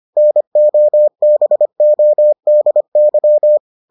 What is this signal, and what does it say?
Morse code, heard as a keyed tone, about 25 wpm, 600 Hz. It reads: NOBODY